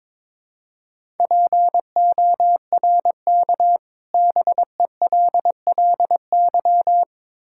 Morse code, heard as a keyed tone, about 22 wpm, 700 Hz. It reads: PORK BELLY
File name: Difficult Words with Hints - 22wpm